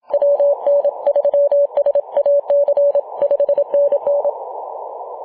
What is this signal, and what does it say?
Morse code, heard as a keyed tone, about 27 wpm, 570 Hz. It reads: WN3SAC 5NN
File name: Mar/26 2219z E51ANQ 18.078MHz CW